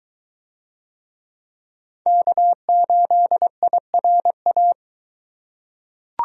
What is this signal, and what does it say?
Morse code, heard as a keyed tone, about 21 wpm, 700 Hz. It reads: K8IRA